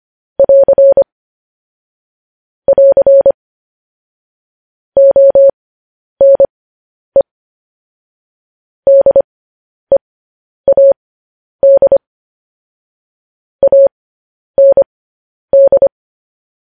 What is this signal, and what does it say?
Morse code, heard as a keyed tone, about 25 wpm, spaced as for 10 wpm, 550 Hz. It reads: <AR> <AR> ONE DEAD AND
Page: News Headlines in Morse Code at 10 WPM